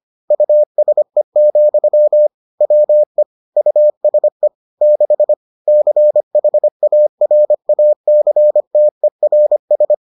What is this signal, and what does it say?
Morse code, heard as a keyed tone, about 25 wpm, 600 Hz. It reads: USE, WE USE 6 CHARACTERS